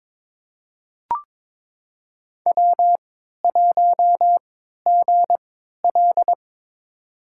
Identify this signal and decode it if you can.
Morse code, heard as a keyed tone, about 22 wpm, spaced as for 13 wpm, 700 Hz. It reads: W1GL
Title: USA Call Signs - 4 Characters - 3x Character Spacing - 22wpm